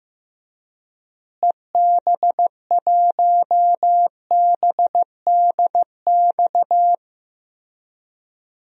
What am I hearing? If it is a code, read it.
Morse code, heard as a keyed tone, about 16 wpm, 700 Hz. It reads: EB1BDX
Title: DXCC-100 Call Signs - 6 Characters - Repeated 2x - 15wpm